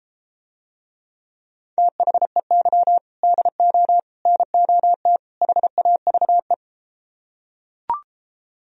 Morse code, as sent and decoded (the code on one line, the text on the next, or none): - .... -.-- / -.. --- / -. --- - / .... .- ...- .
THEY DO NOT HAVE